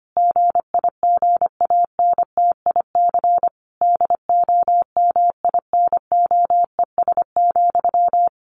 Morse code, heard as a keyed tone, about 25 wpm, 700 Hz. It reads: GIGANTIC DOMINOES,
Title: A Sound Of Thunder 25wpm